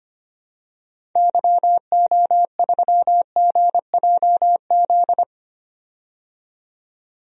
Morse code, heard as a keyed tone, about 25 wpm, 700 Hz. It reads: YO3GJZ
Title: DXCC-100 Call Signs - 6 Characters - 25wpm